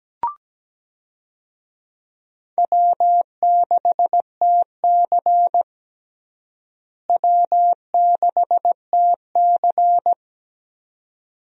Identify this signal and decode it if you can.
Morse code, heard as a keyed tone, about 17 wpm, 700 Hz. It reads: W6TC W6TC
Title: USA Call Signs - 4 Characters - Repeated 2x - 17wpm